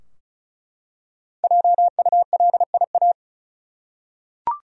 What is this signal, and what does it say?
Morse code, heard as a keyed tone, about 36 wpm, 700 Hz. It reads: JULIA
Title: USA Common Names - Max 5 Letters - 35wpm